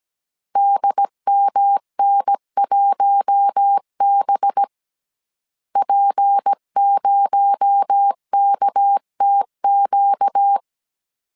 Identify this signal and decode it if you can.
Morse code, heard as a keyed tone, about 17 wpm, 800 Hz. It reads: DMN1B P0KTQ
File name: CW / DET / 1 6 1500C S A 16wpm 2635